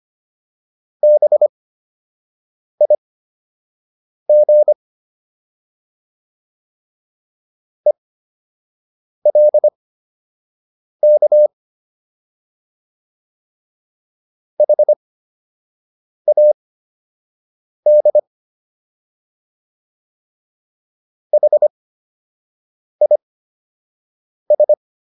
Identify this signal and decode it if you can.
Morse code, heard as a keyed tone, about 25 wpm, spaced as for 6 wpm, 600 Hz. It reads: BIG ELK HAD HIS